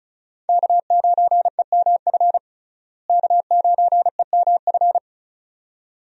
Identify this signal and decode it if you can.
Morse code, heard as a keyed tone, about 34 wpm, 700 Hz. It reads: K9EMF K9EMF
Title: USA Call Signs - 5 Characters - Repeated 2x - 35wpm